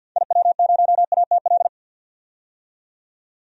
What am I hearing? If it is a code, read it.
Morse code, heard as a keyed tone, about 47 wpm, 700 Hz. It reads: IW0ATL
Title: DXCC-100 Call Signs - 6 Characters - 50wpm